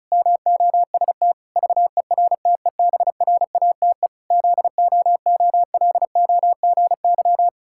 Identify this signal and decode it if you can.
Morse code, heard as a keyed tone, about 36 wpm, 700 Hz. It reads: MOST VERTEBRATE ZOOLOGY
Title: Pangram Challenge v2 - 35wpm